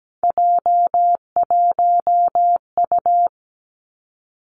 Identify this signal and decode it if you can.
Morse code, heard as a keyed tone, about 17 wpm, 700 Hz. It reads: J1U
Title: Three Challenging Letter-Number Combinations 17wpm